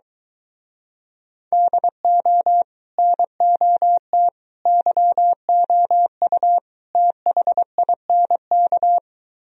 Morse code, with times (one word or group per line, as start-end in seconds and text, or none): 1.5-2.6 DO
3.0-4.3 NOT
4.7-6.6 YOU
7.0-9.0 THINK